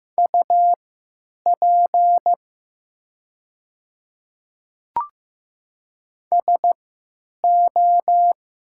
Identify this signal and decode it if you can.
Morse code, heard as a keyed tone, about 15 wpm, spaced as for 9 wpm, 700 Hz. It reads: UP SO
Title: 3x Character Spacing - Top 100 Words - 15wpm